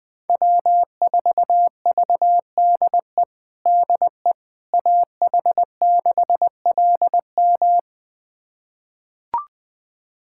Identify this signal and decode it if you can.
Morse code, heard as a keyed tone, about 20 wpm, 700 Hz. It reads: W4VDE DE AH6LM
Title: USA Callsign DE Callsign - 5 Characters - 20wpm